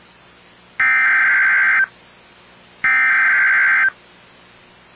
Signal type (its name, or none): Emergency Alert System (SAME) tones